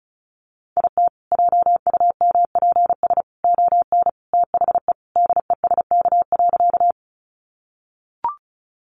Morse code, heard as a keyed tone, about 34 wpm, 700 Hz. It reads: IT JUMPS ON THE DESK.
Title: Additional Easy Phrases 35wpm